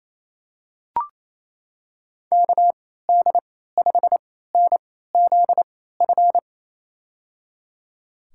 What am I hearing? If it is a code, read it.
Morse code, heard as a keyed tone, about 28 wpm, spaced as for 16 wpm, 700 Hz. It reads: KD5NZF
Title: USA Call Signs - 6 Characters - 3x Character Spacing - 28wpm